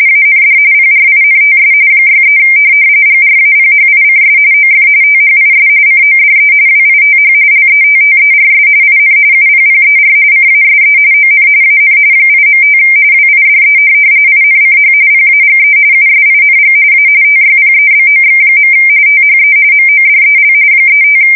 RTTY, about 45.45 baud. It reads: AND PRINTS 63 GRAPHICS OF THIS CODE WITH LETTERS IN UPPER CASE ON AN 8.5 WIDE INCH PAGE USING A TYPEBOX. THE MODEL 35
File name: fsk170 / teletype-corp